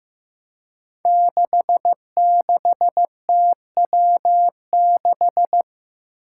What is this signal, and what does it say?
Morse code, heard as a keyed tone, about 15 wpm, 700 Hz. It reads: 66TW6
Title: Five Challenging Letter-Number Combinations 15wpm